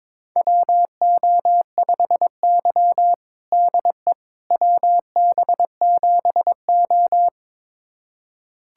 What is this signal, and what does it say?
Morse code, heard as a keyed tone, about 22 wpm, 700 Hz. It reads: WO5Y DE WB7O
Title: USA Callsign DE Callsign - 4 Characters - 22wpm